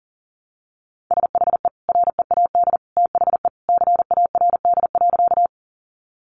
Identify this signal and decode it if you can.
Morse code, heard as a keyed tone, about 40 wpm, 700 Hz. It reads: SHE READ THE CARD.
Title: Sentences from Top 600 Words 40wpm